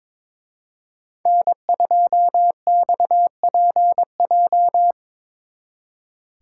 Morse code, heard as a keyed tone, about 24 wpm, 700 Hz. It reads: N2XPJ
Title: Speed Racing - USA 1x3 Call Signs 22wpm